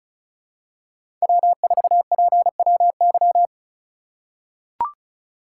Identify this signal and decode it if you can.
Morse code, heard as a keyed tone, about 36 wpm, 700 Hz. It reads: W4PWY
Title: USA Call Signs - 4x Character Spacing - 35wpm